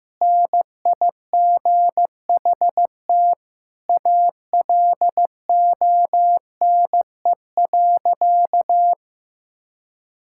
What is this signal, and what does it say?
Morse code, heard as a keyed tone, about 15 wpm, 700 Hz. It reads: NIGHT ALONE.
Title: Sentences from Top 1000 Words - Review Part 5 - 15wpm